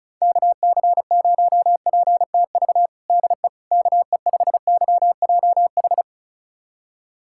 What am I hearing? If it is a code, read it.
Morse code, heard as a keyed tone, about 36 wpm, 700 Hz. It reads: KC0PTV DE KE5YJH
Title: USA Callsign DE Callsign 35wpm